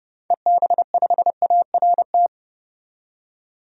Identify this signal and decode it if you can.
Morse code, heard as a keyed tone, about 30 wpm, 700 Hz. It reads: EB5ART
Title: DXCC-100 Call Signs - 6 Characters - 30wpm